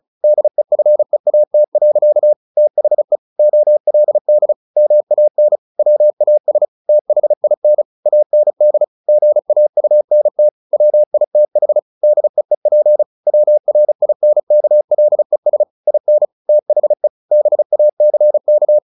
Morse code, heard as a keyed tone, about 34 wpm, 600 Hz. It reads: DEFEAT. THE OLD MAN WAS THIN AND GAUNT WITH DEEP WRINKLES IN THE BACK
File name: the old man and the sea / the old man and the sea 1 35wpm